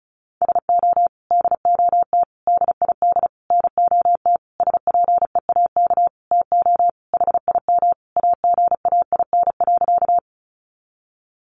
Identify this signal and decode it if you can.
Morse code, heard as a keyed tone, about 34 wpm, 700 Hz. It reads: SO DOT DID NOT SPEAK TO HIM AGAIN.
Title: Sentences from Top 1500 Words 35wpm